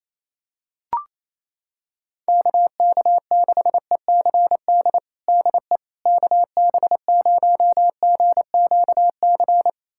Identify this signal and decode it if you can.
Morse code, heard as a keyed tone, about 28 wpm, 700 Hz. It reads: KK6ECD DE KB0GQC
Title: USA Callsign DE Callsign - 6 Characters - 28wpm